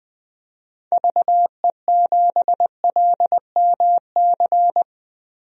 Morse code, heard as a keyed tone, about 20 wpm, 700 Hz. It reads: VE7LMC
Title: DXCC-100 Call Signs - 6 Characters - Repeated 3x - 20wpm